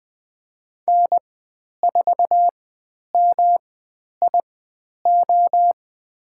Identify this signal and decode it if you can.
Morse code, heard as a keyed tone, about 20 wpm, spaced as for 10 wpm, 700 Hz. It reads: N4MIO
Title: 109 Lesson 18g - Letter m - Call Signs 20wpm